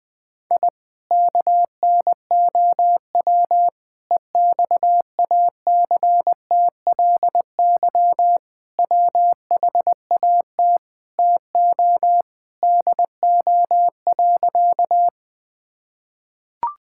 Morse code, heard as a keyed tone, about 20 wpm, 700 Hz. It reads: I KNOW EXACTLY WHAT TO DO.
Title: Sentences from Top 1000 Words - Review Part 6 - 20wpm